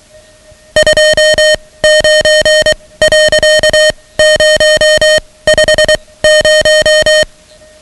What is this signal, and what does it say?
Morse code, heard as a keyed tone, about 23 wpm, spaced as for 17 wpm, 620 Hz, with noise. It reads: 29.050
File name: ham radio / CodeQuick / Code Quick 10